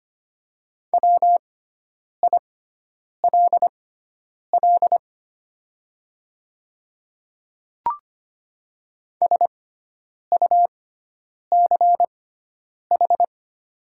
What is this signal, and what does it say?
Morse code, heard as a keyed tone, about 25 wpm, spaced as for 9 wpm, 700 Hz. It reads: WILL SUCH
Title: 6x Character Spacing - Top 300 Words - 25wpm